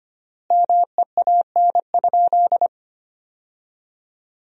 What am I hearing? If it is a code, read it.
Morse code, heard as a keyed tone, about 25 wpm, 700 Hz. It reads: MEAN?
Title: 3x Word-Spacing - Sentences from Top 200 Words - 25wpm